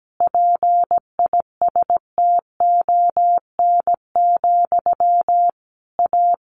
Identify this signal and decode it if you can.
Morse code, heard as a keyed tone, about 17 wpm, 700 Hz. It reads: PISTON, A
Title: A Sound Of Thunder 17wpm